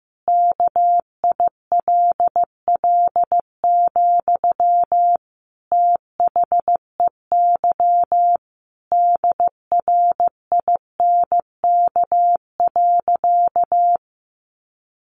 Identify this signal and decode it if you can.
Morse code, heard as a keyed tone, about 15 wpm, 700 Hz. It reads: KILL, THEY DRINK.
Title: Sentences from Top 1500 Words 15wpm